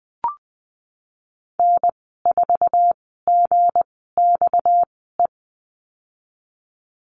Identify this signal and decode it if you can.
Morse code, heard as a keyed tone, about 20 wpm, 700 Hz. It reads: N 4 G X E
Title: 238 Lesson 38g - Letter q - Call Signs 20wpm